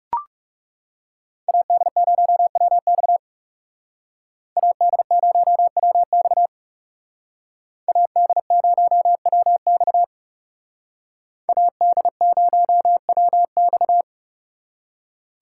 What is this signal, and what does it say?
Morse code, heard as a keyed tone, about 35 wpm, 700 Hz. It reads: AD0WX AD0WX AD0WX AD0WX
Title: Speed Racing - USA 2x2 Call Signs 30wpm